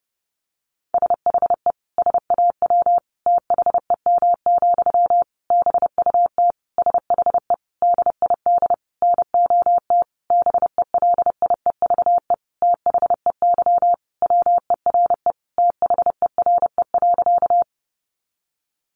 Morse code, heard as a keyed tone, about 28 wpm, 700 Hz. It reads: SHE SAW THEM, BUT SHE DID NOT BELIEVE THEY WERE THERE.
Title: Sentences from Top 1000 Words 30wpm